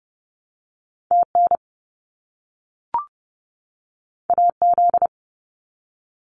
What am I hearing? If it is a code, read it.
Morse code, heard as a keyed tone, about 30 wpm, 700 Hz. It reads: TN AZ
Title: US States 30wpm